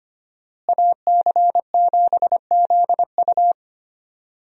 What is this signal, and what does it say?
Morse code, heard as a keyed tone, about 25 wpm, 700 Hz. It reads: AC7ZU
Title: USA Call Signs - 5 Characters - Repeated 3x - 25wpm